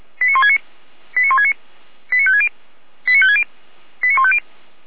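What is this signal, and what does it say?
Five-tone selective call (ZVEI), heard as quick runs of five tones, tones of about 70 ms. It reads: 89169 89169 89560 89560 89150
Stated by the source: ZVEI five-tone selective call received with an SDR, from the audio server of Feuerwehr Orlamünde.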